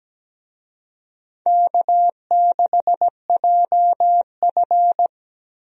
Morse code, heard as a keyed tone, about 17 wpm, 700 Hz. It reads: K6JF
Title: USA Call Signs - 4 Characters - 17wpm